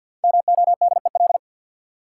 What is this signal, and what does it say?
Morse code, heard as a keyed tone, about 52 wpm, 700 Hz. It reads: MODEL
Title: Sentences from Top 1000 Words - New Word Drill - 50wpm